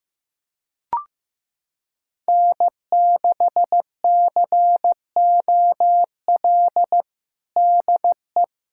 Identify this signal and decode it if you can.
Morse code, heard as a keyed tone, about 15 wpm, 700 Hz. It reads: N6COL DE
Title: USA Callsign DE Callsign - 5 Characters - 15wpm